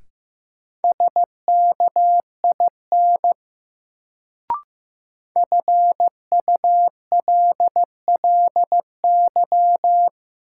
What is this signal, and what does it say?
Morse code, heard as a keyed tone, about 15 wpm, 700 Hz. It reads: SKIN FULLY
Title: Sentences from Top 1300 Words - New Word Drill Alt - 15wpm